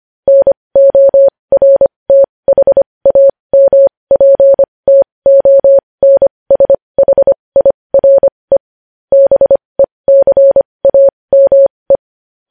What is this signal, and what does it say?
Morse code, heard as a keyed tone, about 25 wpm, spaced as for 20 wpm, 550 Hz. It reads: NORTHAMPTONSHIRE BECAME
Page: News Headlines in Morse Code at 20 WPM